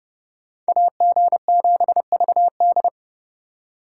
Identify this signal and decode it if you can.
Morse code, heard as a keyed tone, about 40 wpm, 700 Hz. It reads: AG7VD